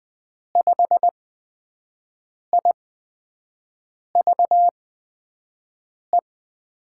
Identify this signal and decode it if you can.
Morse code, heard as a keyed tone, about 20 wpm, spaced as for 5 wpm, 700 Hz. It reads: 5IVE
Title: USA Call Signs - 6 Characters - 8x Character Spacing - 20wpm